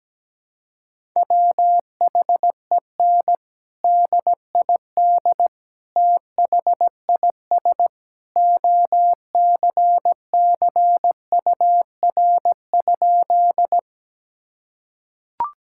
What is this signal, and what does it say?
Morse code, heard as a keyed tone, about 17 wpm, 700 Hz. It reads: WHEN DID THIS OCCUR?
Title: Sentences from Top 1000 Words - Review Part 6 - 17wpm